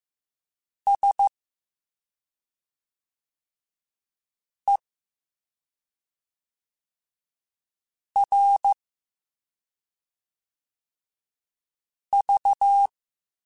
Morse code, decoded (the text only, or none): SERV